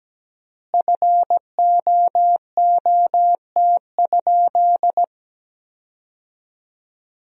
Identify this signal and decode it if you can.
Morse code, heard as a keyed tone, about 17 wpm, 700 Hz. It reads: FOOT?
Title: 3x Word-Spacing - Sentences from Top 500 Words - 17wpm